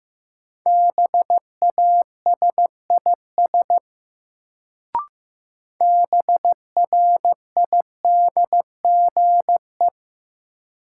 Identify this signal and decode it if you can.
Morse code, heard as a keyed tone, about 16 wpm, 700 Hz. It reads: BASIS BRIDGE
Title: Sentences from Top 1200 Words - New Word Drill Alt - 15wpm